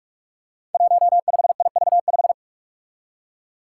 Morse code, heard as a keyed tone, about 46 wpm, 700 Hz. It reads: 15IV5
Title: Five Challenging Letter-Number Combinations 45wpm